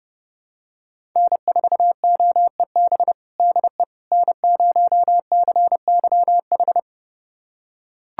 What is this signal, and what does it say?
Morse code, heard as a keyed tone, about 30 wpm, 700 Hz. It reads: N4OEB DE N0CYH
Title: USA Callsign DE Callsign 30wpm